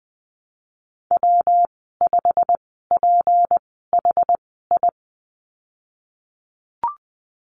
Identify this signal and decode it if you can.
Morse code, heard as a keyed tone, about 20 wpm, spaced as for 15 wpm, 700 Hz. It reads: W5PHI
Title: 161 Lesson 26g - Letter g - Call Signs 20wpm